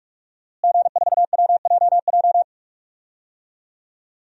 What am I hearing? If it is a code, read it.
Morse code, heard as a keyed tone, about 41 wpm, 700 Hz. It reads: G4WJJ
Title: DXCC-100 Call Signs - 5 Characters - Repeated 3x - 45wpm